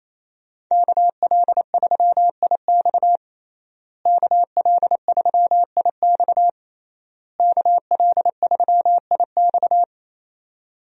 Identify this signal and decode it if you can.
Morse code, heard as a keyed tone, about 28 wpm, 700 Hz. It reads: KL3IX KL3IX KL3IX
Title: USA Call Signs - 5 Characters - Repeated 3x - 28wpm